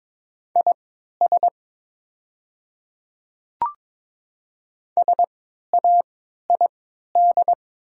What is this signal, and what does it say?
Morse code, heard as a keyed tone, about 22 wpm, spaced as for 12 wpm, 700 Hz. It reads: IS SAID
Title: 3x Character Spacing - Top 100 Words - 22wpm